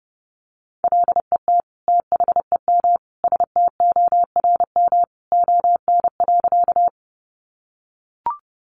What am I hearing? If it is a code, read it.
Morse code, heard as a keyed tone, about 30 wpm, 700 Hz. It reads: LET THEM STORM ON.
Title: Sentences from Top 1500 Words 30wpm